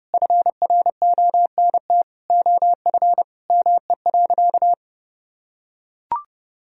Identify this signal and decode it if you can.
Morse code, heard as a keyed tone, about 32 wpm, 700 Hz. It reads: FRONT OF ME.
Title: Sentences from Top 1000 Words - Review Part 2 - 30wpm